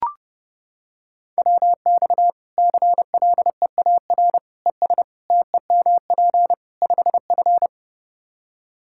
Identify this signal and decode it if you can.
Morse code, heard as a keyed tone, about 30 wpm, 700 Hz. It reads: WX CLEAR ES TEMP 5F
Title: Practice QSO - Exchange 2 - Info - 30wpm